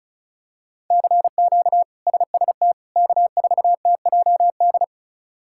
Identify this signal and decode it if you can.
Morse code, heard as a keyed tone, about 36 wpm, 700 Hz. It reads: CQ SST K4TJD
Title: K1USN SST QSO Practice - 35wpm